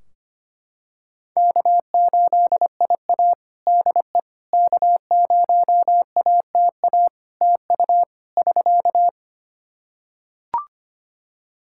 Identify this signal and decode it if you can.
Morse code, heard as a keyed tone, about 25 wpm, 700 Hz. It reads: K8IA DE K0ATA TU <SK>
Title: Practice QSO - Exchange 4 - Ending - 25wpm